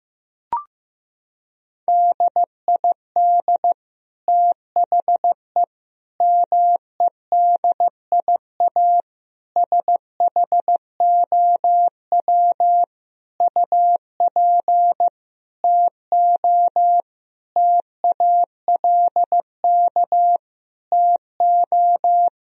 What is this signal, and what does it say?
Morse code, heard as a keyed tone, about 16 wpm, 700 Hz. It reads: DID THE MEDIA SHOW UP TO TALK TO
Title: Sentences from Top 1000 Words - Review Part 4 - 15wpm